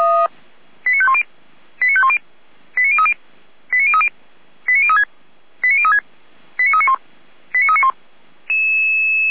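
Five-tone selective call (ZVEI), heard as quick runs of five tones, tones of about 70 ms. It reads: 89510 89510 89030 89030 89036 89036 89391 89391
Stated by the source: Received on an SDR, ZVEI five-tone call.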